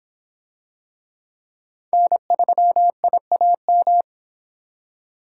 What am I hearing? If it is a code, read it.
Morse code, heard as a keyed tone, about 25 wpm, 700 Hz. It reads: N3IAM